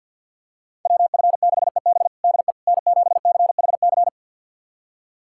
Morse code, heard as a keyed tone, about 47 wpm, 700 Hz. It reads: WF6EB DE N7CHC
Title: USA Callsign DE Callsign - 5 Characters - 50wpm